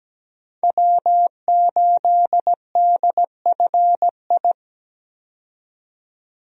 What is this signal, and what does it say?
Morse code, heard as a keyed tone, about 17 wpm, 700 Hz. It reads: W8DFI
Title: USA Call Signs - 5 Characters - 17wpm